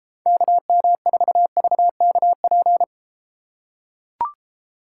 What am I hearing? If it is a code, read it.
Morse code, heard as a keyed tone, about 33 wpm, 700 Hz. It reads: KM4VKP